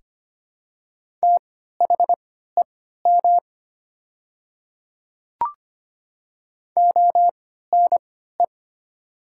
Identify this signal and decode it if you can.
Morse code, heard as a keyed tone, about 25 wpm, spaced as for 14 wpm, 700 Hz. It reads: THEM ONE
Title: 3x Character Spacing - Top 100 Words - 25wpm